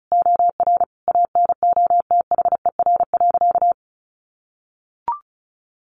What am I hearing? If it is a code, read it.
Morse code, heard as a keyed tone, about 34 wpm, 700 Hz. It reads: OR ANOTHER.
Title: Sentences from Top 1400 Words 35wpm